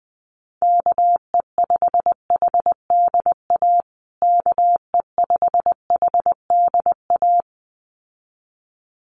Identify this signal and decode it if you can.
Morse code, heard as a keyed tone, about 20 wpm, 700 Hz. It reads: KE5HDA KE5HDA
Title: USA Call Signs Repeated 2x 20wpm